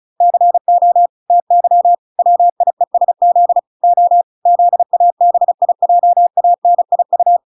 Morse code, heard as a keyed tone, about 34 wpm, 700 Hz. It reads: CO TY WIESZ O ZABIJANIU